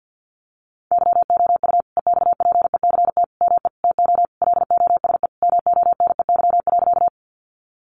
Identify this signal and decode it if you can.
Morse code, heard as a keyed tone, about 50 wpm, 700 Hz. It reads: YOU EXPECT ME TO LOSE MONEY.
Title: Sentences from Top 1100 Words 50wpm